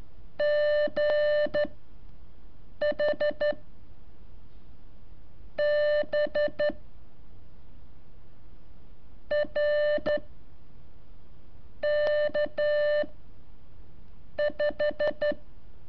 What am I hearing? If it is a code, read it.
Morse code, heard as a keyed tone, about 13 wpm, 590 Hz. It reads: G H B R K 5
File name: curso / ejercicio 20